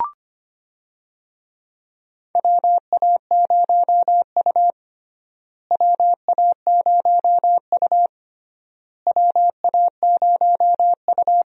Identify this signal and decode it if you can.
Morse code, heard as a keyed tone, about 25 wpm, 700 Hz. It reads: WA0U WA0U WA0U
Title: USA Call Signs - 4 Characters - Repeated 3x - 25wpm